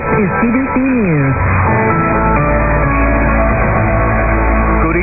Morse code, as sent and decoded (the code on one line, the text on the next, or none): none